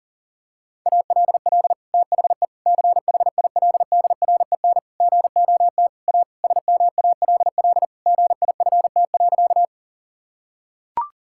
Morse code, heard as a keyed tone, about 40 wpm, 700 Hz. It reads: ALL THE CHILDREN GOT A SMALL GIFT.
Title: Sentences from Top 1000 Words - Review Part 6 - 40wpm